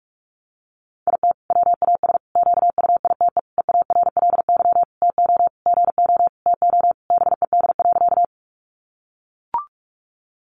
Morse code, heard as a keyed tone, about 44 wpm, 700 Hz. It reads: IT WAS QUITE EARLY TO GO TO BED.
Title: Sentences from Top 700 Words 45wpm